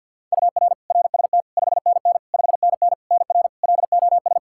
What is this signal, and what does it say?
Morse code, heard as a keyed tone, about 52 wpm, 700 Hz. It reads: UR RST 5NN 5NN NR LOS
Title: K1USN SST QSO Practice 50wpm